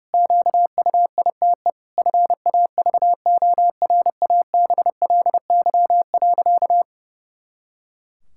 Morse code, heard as a keyed tone, about 30 wpm, 700 Hz. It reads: QUITE FAVORABLY.
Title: Pangram Challenge v2 - 30wpm